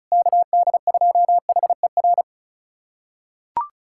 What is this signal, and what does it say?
Morse code, heard as a keyed tone, about 34 wpm, 700 Hz. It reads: KD2HER
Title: USA Callsign DE Callsign - 6 Characters - 35wpm